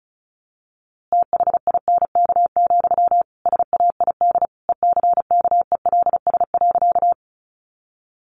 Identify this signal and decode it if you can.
Morse code, heard as a keyed tone, about 34 wpm, 700 Hz. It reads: THINK, SAID ECKELS.
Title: A Sound Of Thunder 35wpm